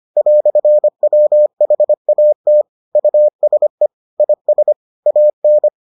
Morse code, heard as a keyed tone, about 25 wpm, 600 Hz. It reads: "WHAT USE IS AN